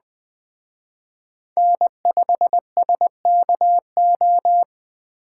Morse code, heard as a keyed tone, about 21 wpm, 700 Hz. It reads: N5SKO
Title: Speed Racing - USA 1x3 Call Signs 17wpm